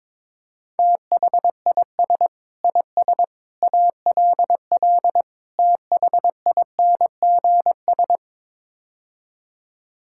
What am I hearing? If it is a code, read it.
Morse code, heard as a keyed tone, about 22 wpm, 700 Hz. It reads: THIS IS ALL THINGS
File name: practice / 22wpm / Build A Phrase - 5-Word Phrases - 22wpm